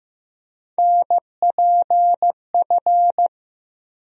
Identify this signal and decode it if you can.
Morse code, heard as a keyed tone, about 15 wpm, 700 Hz. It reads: NPF
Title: USA Call Sign Postfixes - 3 Letters 15wpm